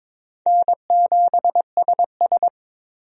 Morse code, heard as a keyed tone, about 22 wpm, 700 Hz. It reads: N7SS
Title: USA Call Signs - 4 Characters - Repeated 2x - 22wpm